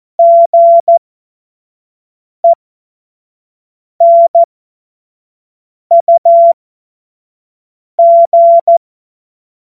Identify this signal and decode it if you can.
Morse code, heard as a keyed tone, about 14 wpm, spaced as for 5 wpm, 680 Hz. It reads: GENUG
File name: Morsekurs / Witze / Witz 30. Mai 2023 25 F70